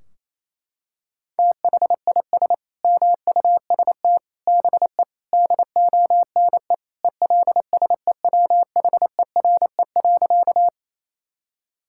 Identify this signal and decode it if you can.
Morse code, heard as a keyed tone, about 28 wpm, 700 Hz. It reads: THIS MUST BE DONE ELSEWHERE.
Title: Sentences from Top 1600 Words 28wpm